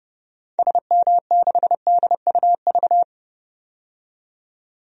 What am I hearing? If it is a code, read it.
Morse code, heard as a keyed tone, about 32 wpm, 700 Hz. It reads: SM6DUV
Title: DXCC-100 Call Signs - 6 Characters - 30wpm